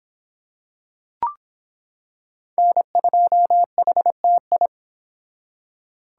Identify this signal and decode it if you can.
Morse code, heard as a keyed tone, about 25 wpm, 700 Hz. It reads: N2HTI